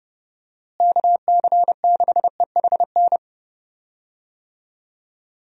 Morse code, heard as a keyed tone, about 30 wpm, 700 Hz. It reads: KC6EHN
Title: USA Call Signs - 6 Characters - Repeated 3x - 30wpm